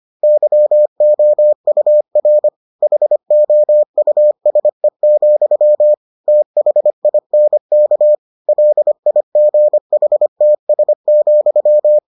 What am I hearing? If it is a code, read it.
Morse code, heard as a keyed tone, about 25 wpm, 600 Hz. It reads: YOUR HOUSE, THINK LIGHTS,